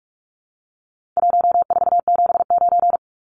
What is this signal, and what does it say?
Morse code, heard as a keyed tone, about 44 wpm, 700 Hz. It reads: 1479
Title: Numbers - Contest Style - 4-Digit 45wpm